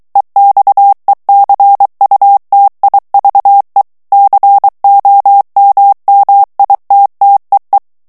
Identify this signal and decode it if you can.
Morse code, heard as a keyed tone, about 23 wpm, 800 Hz. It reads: EXECUTIVE COMMITTEE
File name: ANGLAIS texte clair 3 vitesse 22 mots minute